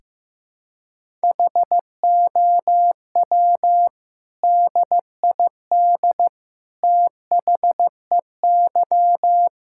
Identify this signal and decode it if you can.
Morse code, heard as a keyed tone, about 16 wpm, 700 Hz. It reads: HOW DID THEY
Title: Sentences from Top 1000 Words - Review Part 5 - 15wpm